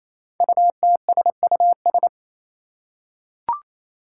Morse code, heard as a keyed tone, about 27 wpm, 700 Hz. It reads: UTSUS
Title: Five Challenging Letter-Number Combinations 28wpm